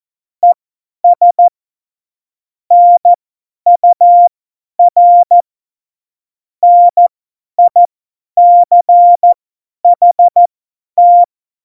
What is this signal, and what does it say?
Morse code, heard as a keyed tone, about 14 wpm, spaced as for 10 wpm, 700 Hz. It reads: ES NUR NICHT
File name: Morsekurs / Witze / Witz 13 March 2018, 50 F70